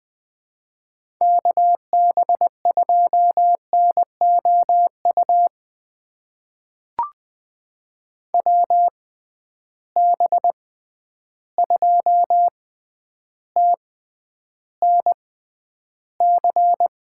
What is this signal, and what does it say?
Morse code, heard as a keyed tone, about 20 wpm, spaced as for 10 wpm, 700 Hz. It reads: KB2NOU WB2TNC
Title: USA Call Signs - 6 Characters - 6x Character Spacing - 20wpm